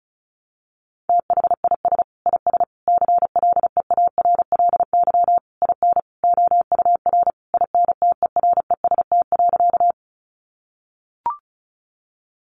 Morse code, heard as a keyed tone, about 34 wpm, 700 Hz. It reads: THIS IS CLEARLY IN OUR INTEREST.
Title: Sentences from Top 1300 Words 35wpm